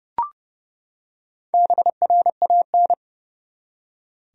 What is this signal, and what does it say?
Morse code, heard as a keyed tone, about 30 wpm, 700 Hz. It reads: BRAN
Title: USA Common Names - 30wpm